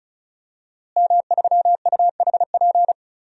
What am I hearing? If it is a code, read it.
Morse code, heard as a keyed tone, about 36 wpm, 700 Hz. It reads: M3UHP
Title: DXCC-100 Call Signs - 5 Characters - Repeated 2x - 35wpm